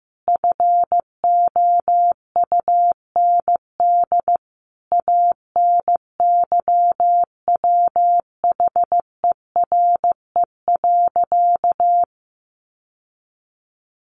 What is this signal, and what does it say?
Morse code, heard as a keyed tone, about 15 wpm, 700 Hz. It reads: FOUND ANYWHERE.
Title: Sentences from Top 1400 Words 15wpm